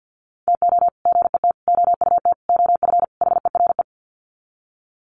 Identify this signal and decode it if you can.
Morse code, heard as a keyed tone, about 50 wpm, 700 Hz. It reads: TO GET OUT OF HERE
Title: Sets of 5 Words Encore 50wpm